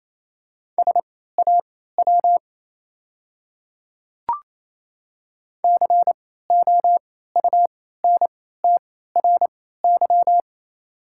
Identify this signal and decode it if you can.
Morse code, heard as a keyed tone, about 28 wpm, spaced as for 16 wpm, 700 Hz. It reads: SAW COUNTRY
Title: 3x Character Spacing - Top 300 Words - 28wpm